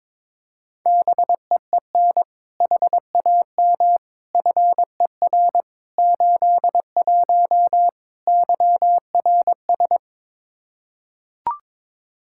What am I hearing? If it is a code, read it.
Morse code, heard as a keyed tone, about 21 wpm, 700 Hz. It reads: BEEN HAM FER 81 YRS
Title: Practice QSO - Exchange 3 - More Info - 22wpm